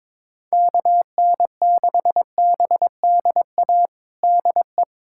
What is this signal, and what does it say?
Morse code, heard as a keyed tone, about 22 wpm, 700 Hz. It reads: KN6BDA DE
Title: USA Callsign DE Callsign 22wpm